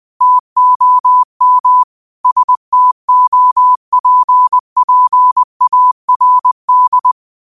Morse code, heard as a keyed tone, about 20 wpm, 1000 Hz. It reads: TOM STOPPARD
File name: qotdfiles / 20wpm